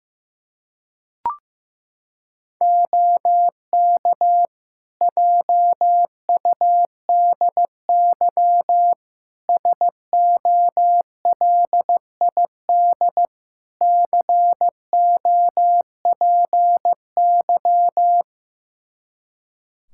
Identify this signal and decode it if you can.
Morse code, heard as a keyed tone, about 15 wpm, 700 Hz. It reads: OK JUDY SOLID COPY
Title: Practice QSO - Exchange 3 - More Info - 15wpm